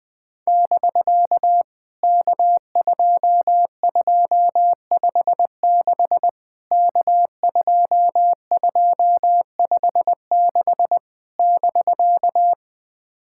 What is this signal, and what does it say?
Morse code, heard as a keyed tone, about 20 wpm, 700 Hz. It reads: <BK> K2256 K2256 <BK>
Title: POTA QSO Practice - 20wpm